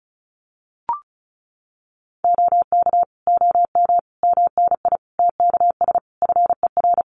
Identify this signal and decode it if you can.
Morse code, heard as a keyed tone, about 34 wpm, 700 Hz. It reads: OK OM MNI TKS FER
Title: Common QSO Phrases 35wpm